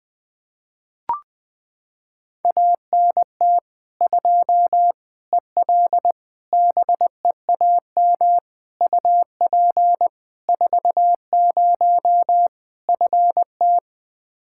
Morse code, heard as a keyed tone, about 20 wpm, 700 Hz. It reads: ANT 2 EL BEAM UP 40 FT
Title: K1USN SST QSO Practice 20wpm